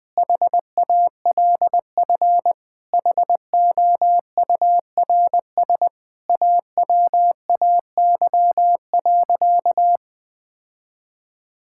Morse code, heard as a keyed tone, about 21 wpm, 700 Hz. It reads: HALF HOURS AWAY.
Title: Sentences with 4-Word Phrases - 20wpm